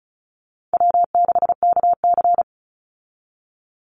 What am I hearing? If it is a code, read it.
Morse code, heard as a keyed tone, about 34 wpm, 700 Hz. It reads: W6KC
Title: Call Signs Repeated 3x - 35wpm